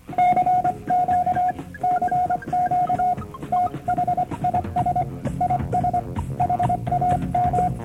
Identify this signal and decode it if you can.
Morse code, heard as a keyed tone, about 24 wpm, 720 Hz, with noise. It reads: CQ CQ THIS IS HAM